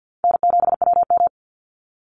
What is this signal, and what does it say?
Morse code, heard as a keyed tone, about 50 wpm, 700 Hz. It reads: N7WM
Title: NA SOTA Chaser Callsigns 50wpm